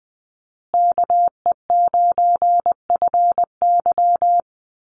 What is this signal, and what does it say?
Morse code, heard as a keyed tone, about 20 wpm, 700 Hz. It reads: KE9FY
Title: USA Call Signs Repeated 2x 20wpm